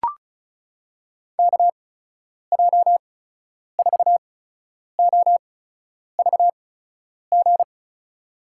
Morse code, heard as a keyed tone, about 36 wpm, spaced as for 10 wpm, 700 Hz. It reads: KJ4OVG
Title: USA Call Signs - 6 Characters - 8x Character Spacing - 35wpm